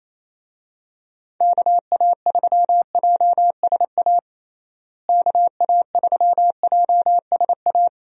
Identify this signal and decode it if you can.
Morse code, heard as a keyed tone, about 28 wpm, 700 Hz. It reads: KA3JSA KA3JSA
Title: USA Call Signs - 6 Characters - Repeated 3x - 28wpm